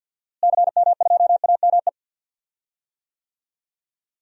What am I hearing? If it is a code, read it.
Morse code, heard as a keyed tone, about 52 wpm, 700 Hz. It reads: CM2AME
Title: DXCC-100 Call Signs - 6 Characters - Repeated 2x - 50wpm